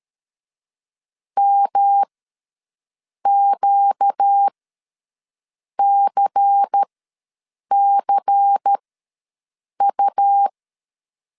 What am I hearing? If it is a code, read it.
Morse code, heard as a keyed tone, about 13 wpm, spaced as for 6 wpm, 800 Hz. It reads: MQCCU
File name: LEZIONI / 5 LBVC / 5 LBVC ws 12wpm 3159